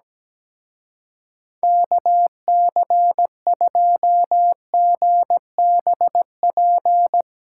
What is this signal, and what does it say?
Morse code, heard as a keyed tone, about 18 wpm, 700 Hz. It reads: KC2GBP
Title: Speed Racing - USA 2x3 Call Signs 17wpm